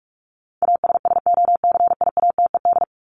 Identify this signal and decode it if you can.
Morse code, heard as a keyed tone, about 44 wpm, 700 Hz. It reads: ASSOCIATED
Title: ED Suffix - Top 100 Words with Base Words 45wpm